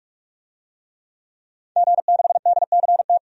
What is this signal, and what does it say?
Morse code, heard as a keyed tone, about 46 wpm, 700 Hz. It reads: G6DCT
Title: DXCC-100 Call Signs - 5 Characters - 45wpm